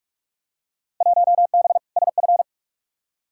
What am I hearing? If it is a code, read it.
Morse code, heard as a keyed tone, about 46 wpm, 700 Hz. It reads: 1B SF
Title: Field Day Exchanges Encore - 45wpm